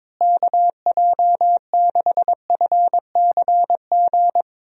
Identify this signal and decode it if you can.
Morse code, heard as a keyed tone, about 22 wpm, 700 Hz. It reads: KJ6FCG
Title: USA Call Signs - 6 Characters - Repeated 3x - 22wpm